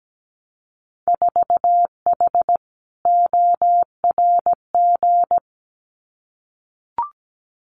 Morse code, heard as a keyed tone, about 17 wpm, 700 Hz. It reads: 4H ORG
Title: Winter Field Day Exchanges Encore 17wpm